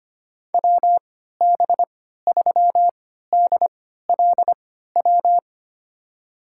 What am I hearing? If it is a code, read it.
Morse code, heard as a keyed tone, about 25 wpm, spaced as for 14 wpm, 700 Hz. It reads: WB3DLW
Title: USA Call Signs - 6 Characters - 3x Character Spacing - 25wpm